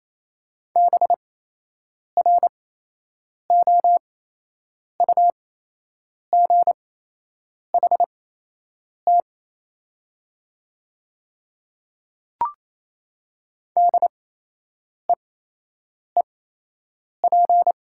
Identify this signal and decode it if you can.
Morse code, heard as a keyed tone, about 28 wpm, spaced as for 8 wpm, 700 Hz. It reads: BROUGHT DEEP
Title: 8x Character Spacing - Top 500 Words - 28wpm